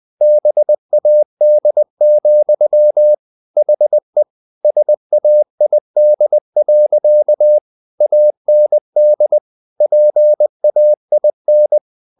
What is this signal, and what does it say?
Morse code, heard as a keyed tone, about 20 wpm, 600 Hz. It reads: BAD, HE SAID. AND PAIN
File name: the old man and the sea part 2 20wpm